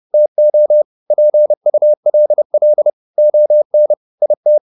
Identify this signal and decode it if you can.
Morse code, heard as a keyed tone, about 30 wpm, 600 Hz. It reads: TO PULL ON IT